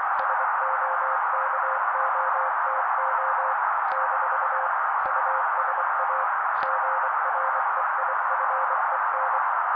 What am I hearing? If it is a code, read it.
Morse code, heard as a keyed tone, about 24 wpm, 530 Hz. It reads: SOKOTO - USA GREIFEN
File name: daily / 23wpm noise